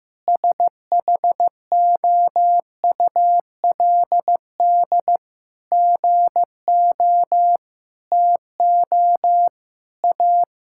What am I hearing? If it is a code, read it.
Morse code, heard as a keyed tone, about 16 wpm, 700 Hz. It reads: SHOULD GO TO A
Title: Sentences from Top 1000 Words - Review Part 5 - 15wpm